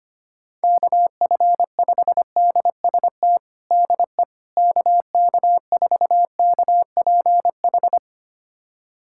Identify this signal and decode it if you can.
Morse code, heard as a keyed tone, about 25 wpm, 700 Hz. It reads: KF5DST DE KK4KPH
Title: USA Callsign DE Callsign 25wpm